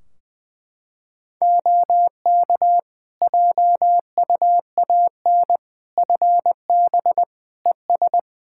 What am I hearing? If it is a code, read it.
Morse code, heard as a keyed tone, about 20 wpm, 700 Hz. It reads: OK JUAN FB ES
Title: K1USN SST QSO Practice 20wpm